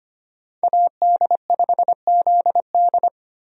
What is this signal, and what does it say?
Morse code, heard as a keyed tone, about 25 wpm, 700 Hz. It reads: AD5ZD
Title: USA Call Signs - 5 Characters - Repeated 3x - 25wpm